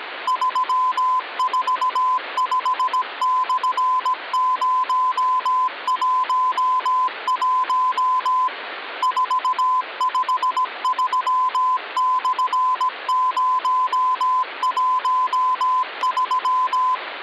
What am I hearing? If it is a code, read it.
Morse code (Russian alphabet), heard as a keyed tone, about 17 wpm, 1020 Hz. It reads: 345/011 453/013
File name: original